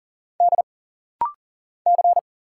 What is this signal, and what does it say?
Morse code, heard as a keyed tone, about 40 wpm, 700 Hz. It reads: D C
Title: Single Letters - Rapid-Fire 40wpm